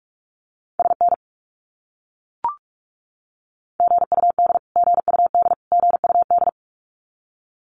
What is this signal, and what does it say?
Morse code, heard as a keyed tone, about 44 wpm, 700 Hz. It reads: SN GUD GUD GUD
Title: Abbreviations Repeated 3x - 45wpm